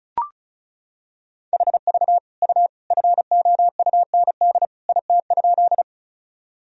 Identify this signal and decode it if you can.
Morse code, heard as a keyed tone, about 36 wpm, 700 Hz. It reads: HV U FOUND IT?
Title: Sentences from Top 300 Words and Abbreviations 35wpm